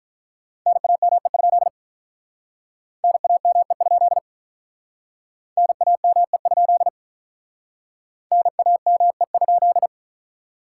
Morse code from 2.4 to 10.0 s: NAME? NAME? NAME?

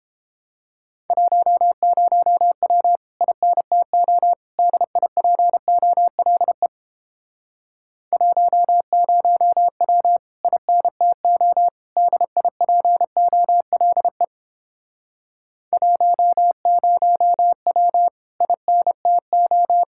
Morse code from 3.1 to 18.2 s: INTO DIPOLE 10W INTO DIPOLE 10W

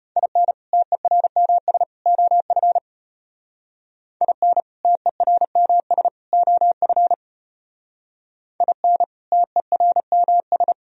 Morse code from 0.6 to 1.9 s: TERMS

